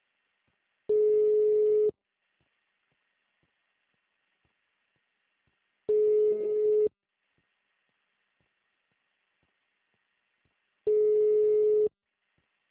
Ringback tone (Europe, 425 Hz).